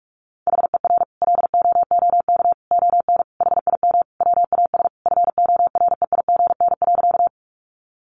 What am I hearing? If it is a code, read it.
Morse code, heard as a keyed tone, about 44 wpm, 700 Hz. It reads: HER LOOK ON HIM WAS FOREIGN.
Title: Sentences from Top 1400 Words 45wpm